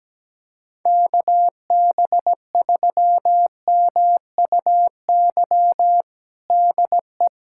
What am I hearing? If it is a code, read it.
Morse code, heard as a keyed tone, about 17 wpm, 700 Hz. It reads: KB3MUY DE
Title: USA Callsign DE Callsign - 6 Characters - 17wpm